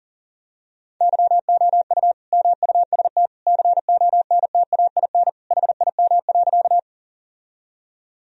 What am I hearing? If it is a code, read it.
Morse code, heard as a keyed tone, about 37 wpm, 700 Hz. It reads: YOU MUST CONTAIN HIM.
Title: Sentences from Top 1000 Words - Review Part 2 - 40wpm